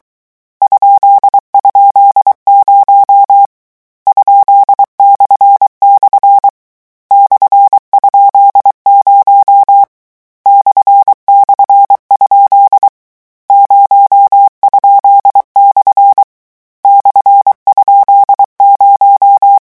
Morse code, heard as a keyed tone, about 23 wpm, 800 Hz. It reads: ??0 ?// /?0 //? 0?/ /?0
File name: morse mp3 / lecon13-vitesse 22